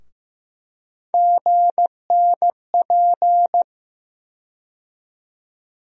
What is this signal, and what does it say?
Morse code, heard as a keyed tone, about 15 wpm, 700 Hz. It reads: GNP
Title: USA Call Sign Postfixes - 3 Letters 15wpm